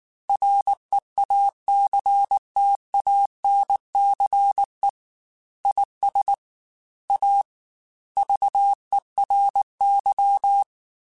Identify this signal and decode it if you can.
Morse code, heard as a keyed tone, about 19 wpm, 800 Hz. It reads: REACTANCE IS A VERY
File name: Technical Subjects / Technical Subjects-mp3s / Reactance @18